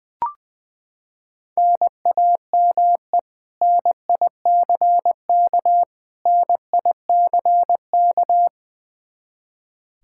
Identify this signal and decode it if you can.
Morse code, heard as a keyed tone, about 21 wpm, 700 Hz. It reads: NAME NICK NICK
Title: Practice QSO - Exchange 1 - Beginning - 20wpm